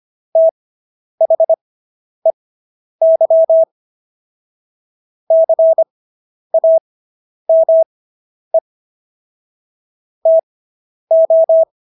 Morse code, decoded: THEY CAME TO

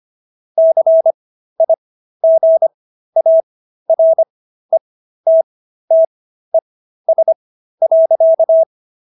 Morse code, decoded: CIGARETTES.

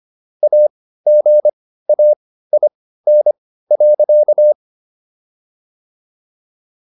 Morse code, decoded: AGAIN.

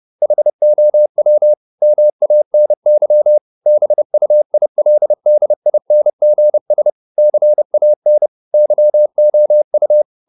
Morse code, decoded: HOW MANY BUILDINGS CAN YOU

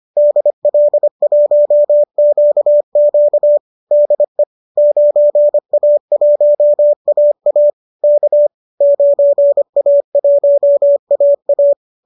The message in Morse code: DL1QQ DE 9A1AA K 9A1AA

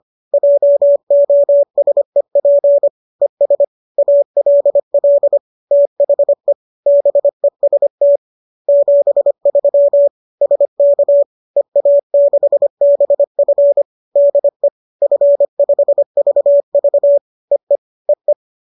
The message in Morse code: JOSEP ES ALL THE BEST 73 SK EA6BF DE F5VV EE EE

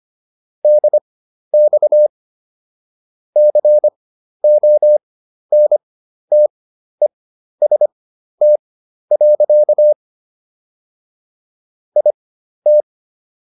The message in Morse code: DX CONTEST. IT

